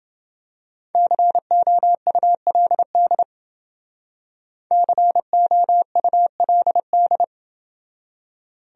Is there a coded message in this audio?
Yes: Morse code from 1.0 to 7.2 s, heard as a keyed tone, about 29 wpm, 700 Hz. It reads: COULD COULD